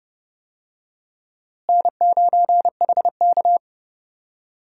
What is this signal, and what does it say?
Morse code, heard as a keyed tone, about 31 wpm, 700 Hz. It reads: N9HK
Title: USA Call Signs - 4 Characters - Repeated 3x - 30wpm